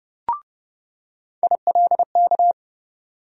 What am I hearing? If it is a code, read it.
Morse code, heard as a keyed tone, about 30 wpm, 700 Hz. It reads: ILK
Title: USA Call Sign Postfixes - 3 Letters 30wpm